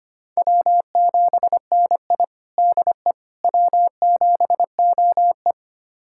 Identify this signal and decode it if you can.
Morse code, heard as a keyed tone, about 25 wpm, 700 Hz. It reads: W7NI DE W7OE
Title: USA Callsign DE Callsign - 4 Characters - 25wpm